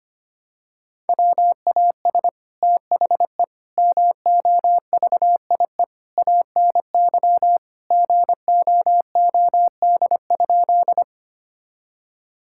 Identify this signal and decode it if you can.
Morse code, heard as a keyed tone, about 25 wpm, 700 Hz. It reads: WAS THE MOVIE ANY GOOD?
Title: Sentences from Top 1000 Words - Review Part 5 - 25wpm